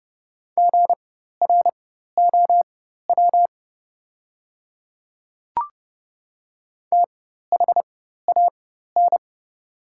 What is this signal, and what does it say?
Morse code, heard as a keyed tone, about 30 wpm, spaced as for 14 wpm, 690 Hz. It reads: GROW THAN
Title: 4x Character Spacing - Top 500 Words - 30wpm